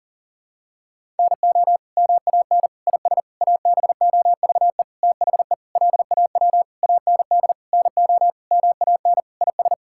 Morse code, heard as a keyed tone, about 40 wpm, 700 Hz. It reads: NO MAN IS ABOVE THE LAW AND NO MAN IS
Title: Sentences from Top 1000 Words - Review Part 5 - 40wpm